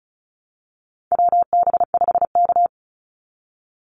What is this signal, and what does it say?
Morse code, heard as a keyed tone, about 34 wpm, 700 Hz. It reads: WB5K
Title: Call Signs Repeated 2x - 35wpm